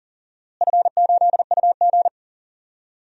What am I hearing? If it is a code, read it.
Morse code, heard as a keyed tone, about 40 wpm, 690 Hz. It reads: F8UG
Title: DXCC-100 Call Signs - 4 Characters - Repeated 2x - 40wpm